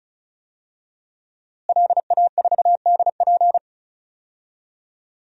Morse code, heard as a keyed tone, about 36 wpm, 700 Hz. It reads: LA4DP
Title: DXCC-100 Call Signs - 5 Characters - Repeated 2x - 35wpm